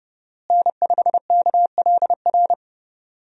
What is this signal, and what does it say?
Morse code, heard as a keyed tone, about 29 wpm, 700 Hz. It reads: N5KLR
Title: USA Call Signs - 5 Characters - Repeated 3x - 30wpm